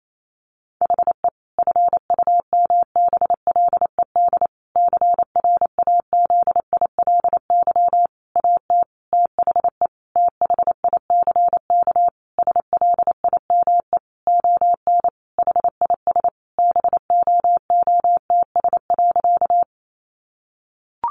Morse code, heard as a keyed tone, about 28 wpm, 700 Hz. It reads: HE FUMBLED CRAZILY AT THE THICK SLIME ON HIS BOOTS.
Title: A Sound Of Thunder 28wpm